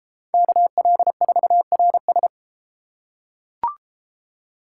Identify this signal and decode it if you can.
Morse code, heard as a keyed tone, about 31 wpm, 700 Hz. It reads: KL4RS